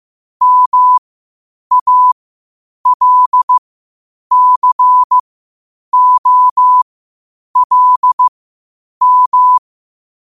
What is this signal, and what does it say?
Morse code, heard as a keyed tone, about 15 wpm, spaced as for 8 wpm, 1000 Hz. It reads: MALCOLM
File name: qotdfiles / 8wpm